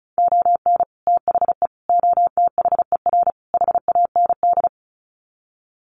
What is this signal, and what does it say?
Morse code, heard as a keyed tone, about 34 wpm, 700 Hz. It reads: ON THE OTHER HAND
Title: Sets of 4 Words 35wpm